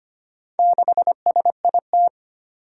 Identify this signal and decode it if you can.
Morse code, heard as a keyed tone, about 25 wpm, 700 Hz. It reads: 6SIT
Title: Four Challenging Letter-Number Combinations 25wpm